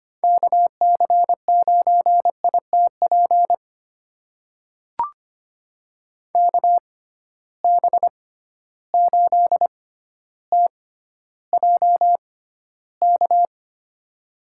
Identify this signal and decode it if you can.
Morse code, heard as a keyed tone, about 25 wpm, spaced as for 13 wpm, 700 Hz. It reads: KC9ITP KB8TJK
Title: USA Call Signs - 6 Characters - 6x Character Spacing - 25wpm